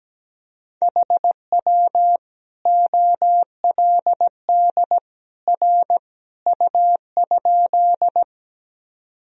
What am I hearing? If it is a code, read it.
Morse code, heard as a keyed tone, about 17 wpm, 700 Hz. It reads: HW OLD R U?
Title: Sentences from Top 200 Words and Abbreviations 17wpm